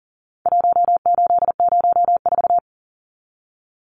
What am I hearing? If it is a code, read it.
Morse code, heard as a keyed tone, about 40 wpm, 700 Hz. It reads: 1804
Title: Numbers - Contest Style - 4-Digit 40wpm